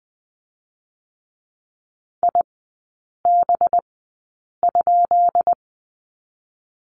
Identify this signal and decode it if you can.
Morse code, heard as a keyed tone, about 20 wpm, spaced as for 8 wpm, 700 Hz. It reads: IB?
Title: 184 Lesson 30e - Letter b - Three Characters 20wpm